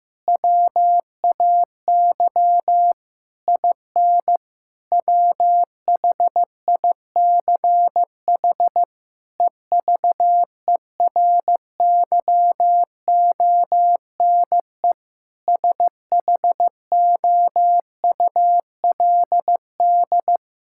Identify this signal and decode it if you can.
Morse code, heard as a keyed tone, about 15 wpm, 700 Hz. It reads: WAY IN WHICH EVERYONE SHOULD
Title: Sentences from Top 1000 Words - Review Part 5 - 15wpm